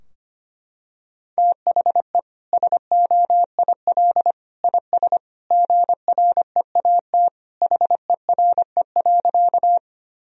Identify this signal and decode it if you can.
Morse code, heard as a keyed tone, about 25 wpm, 700 Hz. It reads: THE SOIL IS GREAT HERE.
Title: Sentences from Top 1000 Words - Review Part 4 - 25wpm